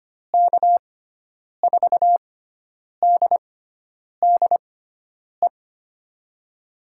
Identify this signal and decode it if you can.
Morse code, heard as a keyed tone, about 25 wpm, spaced as for 9 wpm, 700 Hz. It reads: K4DDE
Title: USA Call Signs - 6 Characters - 6x Character Spacing - 25wpm